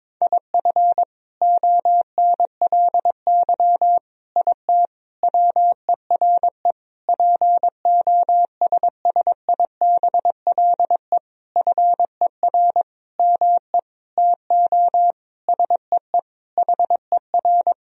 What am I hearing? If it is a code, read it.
Morse code, heard as a keyed tone, about 22 wpm, 700 Hz. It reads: IF ONLY IT WERE POSSIBLE FER ME TO SEE HER
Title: Sentences from Top 500 Words and Abbreviations 22wpm